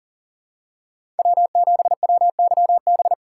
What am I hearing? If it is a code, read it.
Morse code, heard as a keyed tone, about 37 wpm, 700 Hz. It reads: W7WYB
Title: USA Call Signs - 5 Characters - 4x Character Spacing - 40wpm